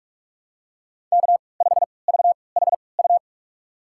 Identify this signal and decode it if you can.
Morse code, heard as a keyed tone, about 44 wpm, spaced as for 25 wpm, 700 Hz. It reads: K5VHU
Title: USA Call Signs - 5 Characters - 3x Character Spacing - 45wpm